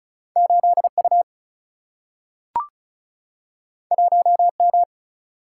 Morse code, heard as a keyed tone, about 36 wpm, 700 Hz. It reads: 8U 1M
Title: Single-Letter Single-Number 35wpm